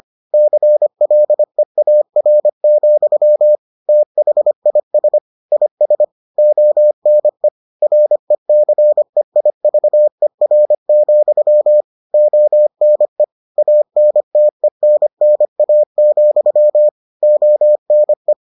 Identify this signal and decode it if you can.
Morse code, heard as a keyed tone, about 25 wpm, 600 Hz. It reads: CLEAR, THIS IS ONE RECEIVER, ONE ANTENNA, ONE